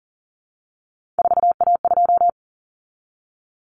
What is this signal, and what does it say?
Morse code, heard as a keyed tone, about 40 wpm, 700 Hz. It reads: 4A2
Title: Common Cut Numbers - 3-Digit 40wpm